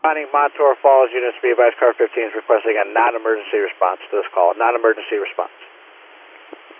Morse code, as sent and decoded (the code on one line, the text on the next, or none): none